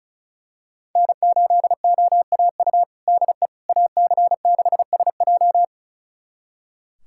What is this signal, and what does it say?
Morse code, heard as a keyed tone, about 34 wpm, 700 Hz. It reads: N8OAU DE AC6SJ
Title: USA Callsign DE Callsign - 5 Characters - 35wpm